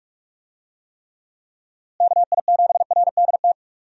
Morse code, heard as a keyed tone, about 46 wpm, 700 Hz. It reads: KI7RDT
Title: USA Call Signs - 6 Characters - Repeated 2x - 45wpm